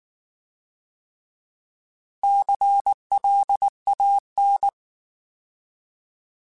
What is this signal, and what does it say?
Morse code, heard as a keyed tone, about 19 wpm, 800 Hz. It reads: CLAN
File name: Alphabetical / Alphabetical mp3s / Cs @18